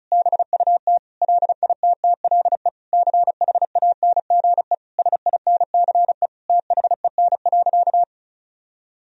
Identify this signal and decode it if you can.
Morse code, heard as a keyed tone, about 36 wpm, 700 Hz. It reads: BUT LITTLE CHANGE SINCE THEN.
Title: Sentences from Top 1000 Words - Review Part 1 - 35wpm